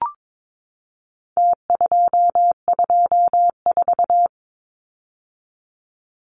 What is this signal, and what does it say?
Morse code, heard as a keyed tone, about 22 wpm, 700 Hz. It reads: T224
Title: Numbers - Contest Style - 4-Digit 22wpm